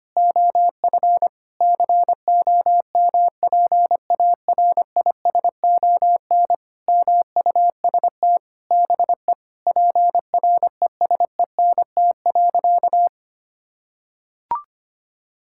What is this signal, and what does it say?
Morse code, heard as a keyed tone, about 25 wpm, 700 Hz. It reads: OF COMPARISON MUST BE PRESENT.
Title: Sentences from Top 1600 Words 25wpm